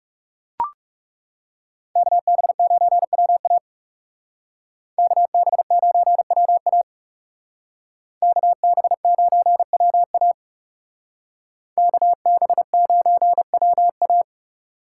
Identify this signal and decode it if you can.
Morse code, heard as a keyed tone, about 39 wpm, 700 Hz. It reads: KB9WA KB9WA KB9WA KB9WA